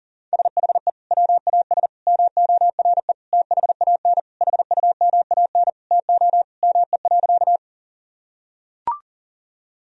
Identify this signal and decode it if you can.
Morse code, heard as a keyed tone, about 40 wpm, 700 Hz. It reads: SHE WAS MORE THAN HUMAN TO ME.
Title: Sentences from Top 1000 Words - Review Part 3 - 40wpm